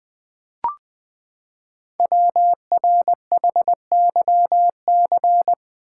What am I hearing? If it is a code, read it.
Morse code, heard as a keyed tone, about 20 wpm, 700 Hz. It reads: WRHYC
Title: Code Groups - 5 Letters - 20wpm